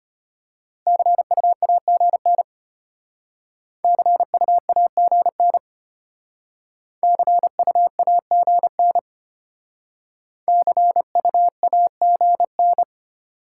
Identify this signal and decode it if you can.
Morse code, heard as a keyed tone, about 31 wpm, 690 Hz. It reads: CUAGN CUAGN CUAGN CUAGN
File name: Speed-Racing Top 20 QSO Elements - 25wpm